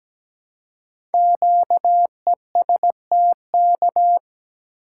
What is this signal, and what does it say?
Morse code, heard as a keyed tone, about 17 wpm, 700 Hz. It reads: QESTK
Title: Code Groups - 5 Letters - 17wpm